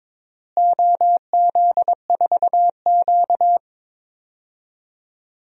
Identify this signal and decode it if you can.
Morse code, heard as a keyed tone, about 22 wpm, 700 Hz. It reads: OZ4Q
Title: DXCC-100 Call Signs - 4 Characters - Repeated 3x - 22wpm